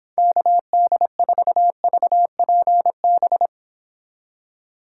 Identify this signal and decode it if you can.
Morse code, heard as a keyed tone, about 26 wpm, 700 Hz. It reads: KD4VPB